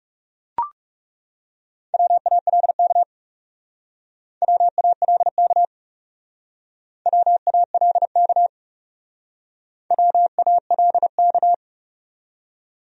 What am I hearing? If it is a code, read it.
Morse code, heard as a keyed tone, about 36 wpm, 700 Hz. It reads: WALK WALK WALK WALK
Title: Speed-Racing Top 500 Words 30wpm